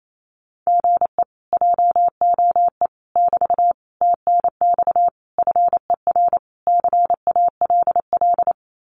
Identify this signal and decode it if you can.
Morse code, heard as a keyed tone, about 28 wpm, 700 Hz. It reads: GE JOE <BT> TNX FER CALL
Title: Common QSO Phrases Encore 28wpm